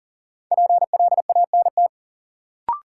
Morse code, heard as a keyed tone, about 39 wpm, 700 Hz. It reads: PLANT
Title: Sentences from Top 1000 Words - Review Part 1 - New Word Drill - 40wpm